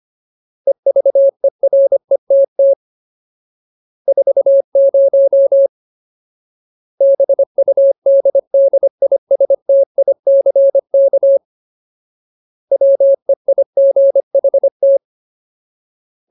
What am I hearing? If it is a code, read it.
Morse code, heard as a keyed tone, about 25 wpm, 550 Hz. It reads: EVERETT 40 BUDDISTICK WEIGHT
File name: Ultimate Word List / 25WPM Ultimate